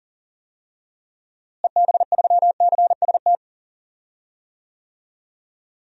Morse code, heard as a keyed tone, about 40 wpm, 700 Hz. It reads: EB3CST
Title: DXCC-100 Call Signs - 6 Characters - Repeated 2x - 40wpm